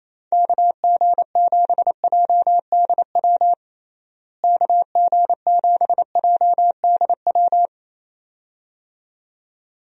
Morse code, heard as a keyed tone, about 28 wpm, 700 Hz. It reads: KG7JDW KG7JDW
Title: USA Call Signs - 6 Characters - Repeated 2x - 28wpm